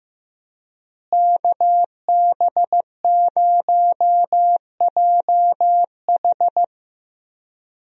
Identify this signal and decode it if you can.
Morse code, heard as a keyed tone, about 16 wpm, 700 Hz. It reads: KB0JH
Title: USA Call Signs - 5 Characters - Repeated 3x - 15wpm